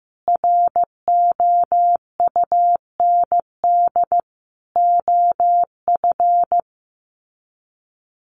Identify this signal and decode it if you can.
Morse code, heard as a keyed tone, about 15 wpm, 700 Hz. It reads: ROUND OF
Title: Sets of 5 Words Encore 15wpm